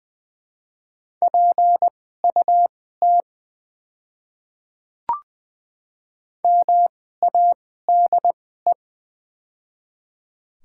Morse code, heard as a keyed tone, about 20 wpm, spaced as for 14 wpm, 700 Hz. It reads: PUT MADE
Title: 2x Character Spacing - Top 200 Words - 20wpm